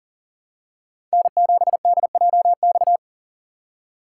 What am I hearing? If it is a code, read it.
Morse code, heard as a keyed tone, about 40 wpm, 700 Hz. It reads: N7DJX
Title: USA Call Signs - 5 Characters - 6x Character Spacing - 40wpm